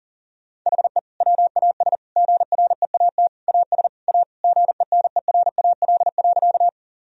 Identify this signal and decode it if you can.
Morse code, heard as a keyed tone, about 40 wpm, 700 Hz. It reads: HE WAS GREAT AS A GENERAL.
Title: Sentences from Top 1000 Words - Review Part 3 - 40wpm